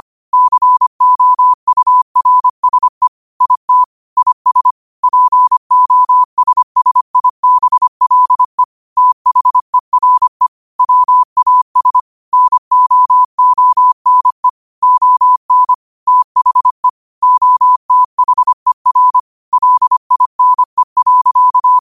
Morse code, heard as a keyed tone, about 25 wpm, 1000 Hz. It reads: COURSE IT IS POSSIBLE THERE WAS NOONE ON THE OTHER LINE.
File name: qotdfiles / 25wpm